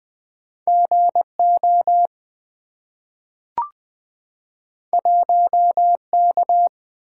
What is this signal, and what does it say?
Morse code, heard as a keyed tone, about 20 wpm, 700 Hz. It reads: G O 1 K
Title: Two Letter-Number Combinations - 20wpm